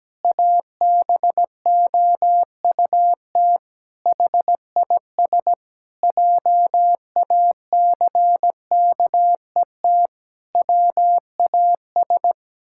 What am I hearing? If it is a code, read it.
Morse code, heard as a keyed tone, about 17 wpm, 700 Hz. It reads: ABOUT HIS JACKET WAS
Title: Pangram Challenge v2 - 17wpm